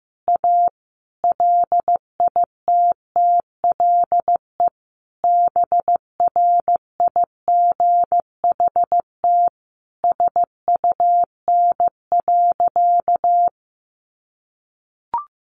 Morse code, heard as a keyed tone, about 15 wpm, 700 Hz. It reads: A LITTLE BRIGHT SUN.
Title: Sentences from Top 1000 Words 15wpm